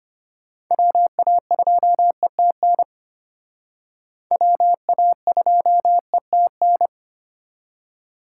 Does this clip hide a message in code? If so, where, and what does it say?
Yes: Morse code from 0.7 to 6.9 s, heard as a keyed tone, about 31 wpm, 700 Hz. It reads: WA2ETN WA2ETN